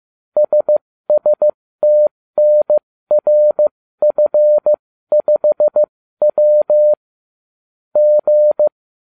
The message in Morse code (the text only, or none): SSTNRF5W G